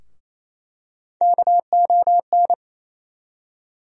Morse code, heard as a keyed tone, about 28 wpm, 700 Hz. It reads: KON
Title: USA Call Sign Postfixes - 3 Letters 28wpm